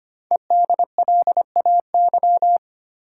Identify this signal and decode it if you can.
Morse code, heard as a keyed tone, about 25 wpm, 690 Hz. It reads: EDLAY
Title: Code Groups - 5 Letters - 25wpm